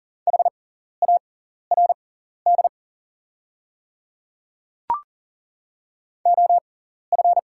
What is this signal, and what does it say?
Morse code, heard as a keyed tone, about 39 wpm, spaced as for 14 wpm, 700 Hz. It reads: HARD OF
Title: 6x Character Spacing - Top 500 Words - 40wpm